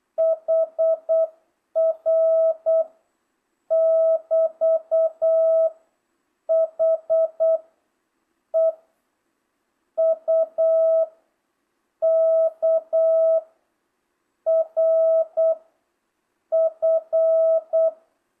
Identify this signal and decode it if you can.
Morse code, heard as a keyed tone, about 8 wpm, 650 Hz. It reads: HR <BT> H E U K R F